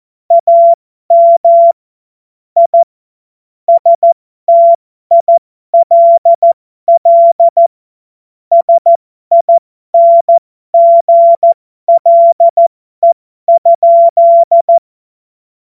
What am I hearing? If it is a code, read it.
Morse code, heard as a keyed tone, about 14 wpm, spaced as for 12 wpm, 680 Hz. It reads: AM I STILL SINGLE?
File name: Morsekurs / Witze